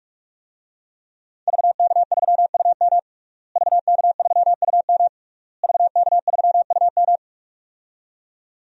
Morse code, heard as a keyed tone, about 44 wpm, 700 Hz. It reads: VK3UM VK3UM VK3UM
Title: DXCC-100 Call Signs - 5 Characters - Repeated 3x - 45wpm